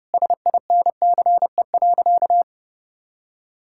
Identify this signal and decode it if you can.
Morse code, heard as a keyed tone, about 30 wpm, 700 Hz. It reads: SINCE.
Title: Sentences from Top 1000 Words - Review Part 5 - 30wpm